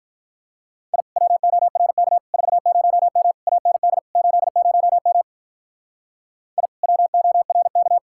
Morse code, heard as a keyed tone, about 44 wpm, 700 Hz. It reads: I WORK 40M AND 80M I WORK
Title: Speed-Racing Common QSO Phrases Encore - 35wpm